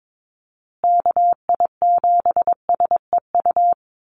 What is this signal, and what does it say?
Morse code, heard as a keyed tone, about 22 wpm, 700 Hz. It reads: KI7SEU
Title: USA Call Signs Repeated 3x 22wpm